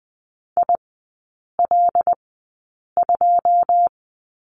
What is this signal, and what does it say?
Morse code, heard as a keyed tone, about 20 wpm, spaced as for 8 wpm, 700 Hz. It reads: IL2
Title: 081 Lesson 14e - Number 2 - Three Characters 20wpm